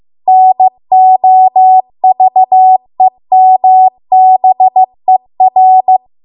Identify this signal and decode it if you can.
Morse code, heard as a keyed tone, about 15 wpm, 750 Hz. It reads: NOVEMBER
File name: Archive / 15 WPM / 15WPM